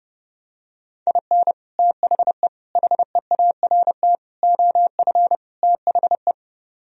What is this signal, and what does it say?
Morse code, heard as a keyed tone, about 32 wpm, 700 Hz. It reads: IN THE HEART OF THE